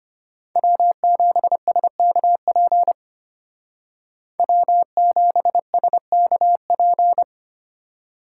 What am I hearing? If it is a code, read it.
Morse code, heard as a keyed tone, about 30 wpm, 700 Hz. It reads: W7SKP W7SKP
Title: Speed Racing - USA Call Signs 25wpm